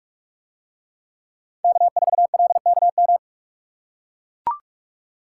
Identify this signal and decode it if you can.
Morse code, heard as a keyed tone, about 44 wpm, 700 Hz. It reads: K4LKM